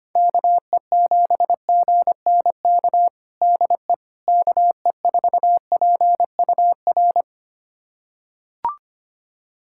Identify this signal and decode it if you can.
Morse code, heard as a keyed tone, about 25 wpm, 700 Hz. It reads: KE7GNK DE KE4PUR
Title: USA Callsign DE Callsign - 6 Characters - 25wpm